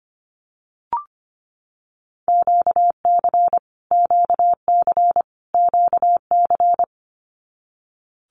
Morse code, heard as a keyed tone, about 25 wpm, 700 Hz. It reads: QC QC QC
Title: Canadian Provinces and Territories - Repeated 3x - 25wpm